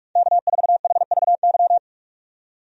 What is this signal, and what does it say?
Morse code, heard as a keyed tone, about 47 wpm, 700 Hz. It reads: K4HVY
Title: USA Call Signs - 5 Characters - 45wpm